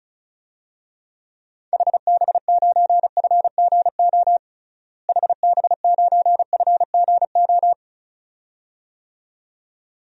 Morse code, heard as a keyed tone, about 34 wpm, 700 Hz. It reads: HB9FGO HB9FGO
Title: DXCC-100 Call Signs - 6 Characters - Repeated 2x - 35wpm